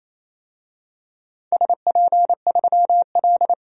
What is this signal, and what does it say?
Morse code, heard as a keyed tone, about 27 wpm, 700 Hz. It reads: SP3L
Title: DXCC-100 Call Signs - 4 Characters - 28wpm